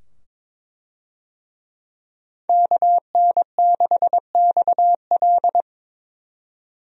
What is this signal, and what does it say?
Morse code, heard as a keyed tone, about 22 wpm, 700 Hz. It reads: KN6XL
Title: USA Call Signs - 5 Characters - 22wpm